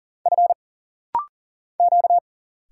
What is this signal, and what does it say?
Morse code, heard as a keyed tone, about 40 wpm, 700 Hz. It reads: F Q
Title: Single Letter-Number - Rapid-Fire 40wpm